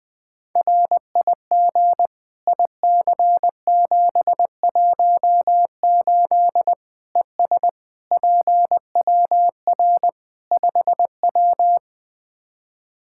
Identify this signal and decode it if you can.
Morse code, heard as a keyed tone, about 21 wpm, 700 Hz. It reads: RIG IC718 ES PWR 5W
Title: Practice QSO - Exchange 2 - Info - 20wpm